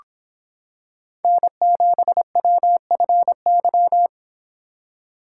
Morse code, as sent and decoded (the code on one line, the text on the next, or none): -. --... .-- ..-. -.--
N7WFY